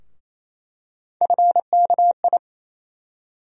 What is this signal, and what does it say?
Morse code, heard as a keyed tone, about 28 wpm, 700 Hz. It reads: FKI
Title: USA Call Sign Postfixes - 3 Letters 28wpm